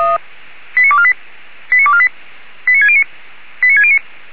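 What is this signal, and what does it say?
Five-tone selective call (ZVEI), heard as quick runs of five tones, tones of about 70 ms. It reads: 89248 89248 89609 89609
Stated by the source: ZVEI five-tone call received by SDR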